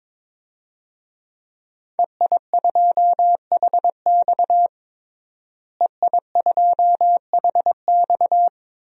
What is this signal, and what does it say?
Morse code, heard as a keyed tone, about 22 wpm, 700 Hz. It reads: EI2HX EI2HX
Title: DXCC-100 Call Signs - 5 Characters - Repeated 2x - 22wpm